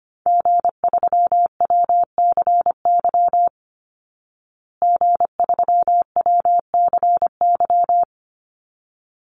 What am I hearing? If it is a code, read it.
Morse code, heard as a keyed tone, about 25 wpm, 700 Hz. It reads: G3WCY G3WCY
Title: Call Signs Repeated 3x - 25wpm